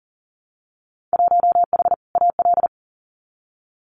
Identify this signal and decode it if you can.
Morse code, heard as a keyed tone, about 40 wpm, 700 Hz. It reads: 1H AL
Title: Winter Field Day Exchanges 40wpm